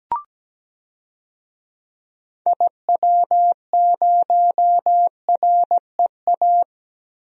Morse code, heard as a keyed tone, about 17 wpm, 700 Hz. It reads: IW0REA
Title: DXCC-100 Call Signs - 6 Characters - Repeated 2x - 17wpm